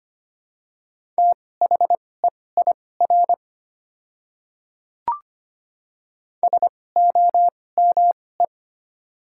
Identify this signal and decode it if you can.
Morse code, heard as a keyed tone, about 25 wpm, spaced as for 18 wpm, 700 Hz. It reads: THEIR SOME
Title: 2x Character Spacing - Top 200 Words - 25wpm